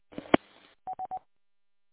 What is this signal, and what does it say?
There is no code.